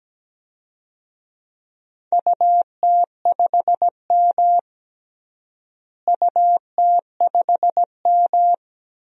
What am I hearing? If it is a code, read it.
Morse code, heard as a keyed tone, about 17 wpm, 700 Hz. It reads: UT5M UT5M
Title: DXCC-100 Call Signs - 4 Characters - Repeated 3x - 17wpm